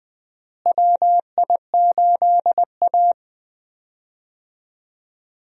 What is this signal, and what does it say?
Morse code, heard as a keyed tone, about 19 wpm, 700 Hz. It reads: WI8A
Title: USA Call Signs - 4 Characters - Repeated 3x - 20wpm